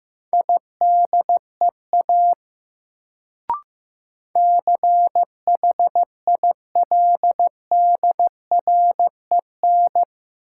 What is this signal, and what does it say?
Morse code, heard as a keyed tone, about 16 wpm, 700 Hz. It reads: IDEA CHILDREN
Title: Sentences from Top 300 Words - New Word Drill - 15wpm